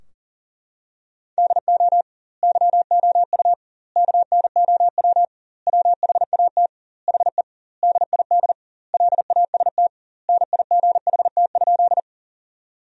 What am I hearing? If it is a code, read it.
Morse code, heard as a keyed tone, about 40 wpm, 700 Hz. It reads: DO YOU KNOW WHAT HE DID LAST NIGHT?
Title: 2x Word-Spacing - Sentences from Top 300 Words - 40wpm